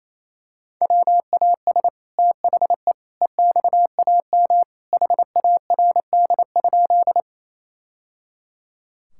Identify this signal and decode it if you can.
Morse code, heard as a keyed tone, about 28 wpm, 700 Hz. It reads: WAS THE EXAM HARD?
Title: Sentences from Top 1000 Words - Review Part 5 - 28wpm